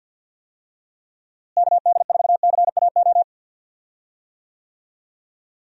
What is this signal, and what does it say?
Morse code, heard as a keyed tone, about 50 wpm, 700 Hz. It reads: KD4CAO
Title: USA Call Signs - 6 Characters - Repeated 2x - 50wpm